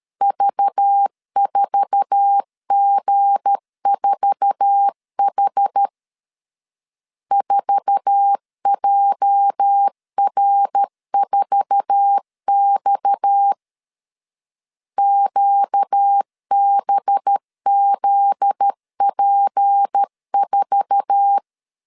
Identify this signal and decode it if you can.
Morse code, heard as a keyed tone, about 13 wpm, 800 Hz. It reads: V4G4H 4JR4X QBZP4
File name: LEZIONI / 8 YJ45 / 8 YJ4 12wpm 2151